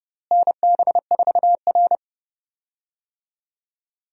Morse code, heard as a keyed tone, about 30 wpm, 700 Hz. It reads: NB4R
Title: USA Call Signs - 4 Characters - Repeated 2x - 30wpm